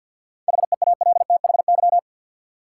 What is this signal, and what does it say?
Morse code, heard as a keyed tone, about 50 wpm, 700 Hz. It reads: HEALTHY
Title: Seven Letter Words - 50wpm